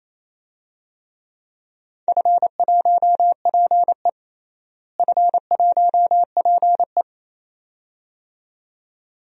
Morse code, heard as a keyed tone, about 28 wpm, 700 Hz. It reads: F1PE F1PE
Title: DXCC-100 Call Signs - 4 Characters - Repeated 2x - 28wpm